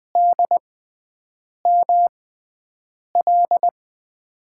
Morse code, heard as a keyed tone, about 20 wpm, spaced as for 7 wpm, 700 Hz. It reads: DML
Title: USA Call Signs - 5 Characters - 6x Character Spacing - 20wpm